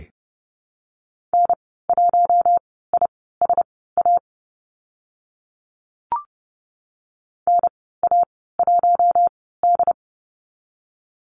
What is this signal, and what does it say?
Morse code, heard as a keyed tone, about 30 wpm, spaced as for 17 wpm, 700 Hz. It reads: N1ISA NA1D
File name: Lesson 15g - Number 5 - Call Signs 30wpm